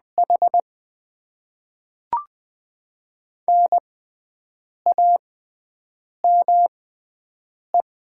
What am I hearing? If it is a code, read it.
Morse code, heard as a keyed tone, about 20 wpm, spaced as for 7 wpm, 690 Hz. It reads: H NAME
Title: Top 20 QSO Elements - 6x Character Spacing - 20wpm